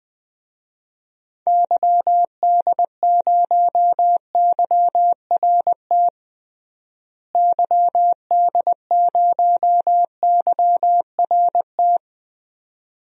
Morse code, heard as a keyed tone, about 20 wpm, 700 Hz. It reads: YD0YRT YD0YRT
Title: DXCC-100 Call Signs - 6 Characters - Repeated 2x - 20wpm